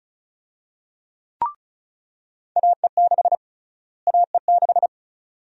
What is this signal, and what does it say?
Morse code, heard as a keyed tone, about 34 wpm, 700 Hz. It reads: AE6 AE6
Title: USA Call Sign Prefixes - Repeated 2x - 35wpm